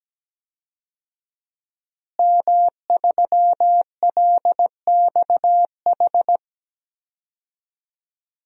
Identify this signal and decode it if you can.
Morse code, heard as a keyed tone, about 17 wpm, 700 Hz. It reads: M3LXH
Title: DXCC-100 Call Signs - 5 Characters - Repeated 2x - 17wpm